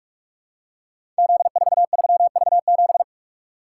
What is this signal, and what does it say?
Morse code, heard as a keyed tone, about 44 wpm, 700 Hz. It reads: Z43V7
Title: Code Groups - 5 Letter:Numbers - 45wpm